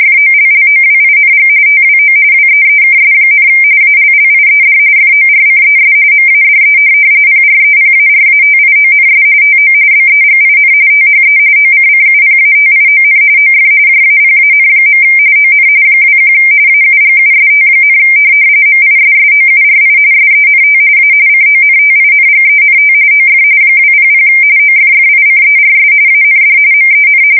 RTTY, about 45.45 baud. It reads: FOLLOWING TWO WEEKS, ALLIED FORCES MADE PREPARATIONS TO BEGIN THE OCCUPATION OF JAPAN. ON 21 AUGUST, MISSOURI SENT A CONTINGENT OF 200 OFFICERS AND MEN
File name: fsk170 / USS-Missouri